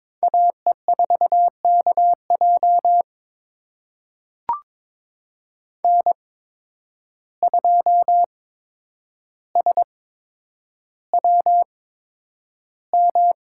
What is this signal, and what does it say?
Morse code, heard as a keyed tone, about 21 wpm, spaced as for 9 wpm, 700 Hz. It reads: AE4KJ N2SWM
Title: USA Call Signs - 5 Characters - 8x Character Spacing - 22wpm